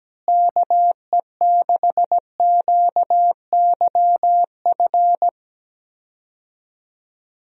Morse code, heard as a keyed tone, about 17 wpm, 700 Hz. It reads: KE6QYF
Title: USA Call Signs - 6 Characters - Repeated 3x - 17wpm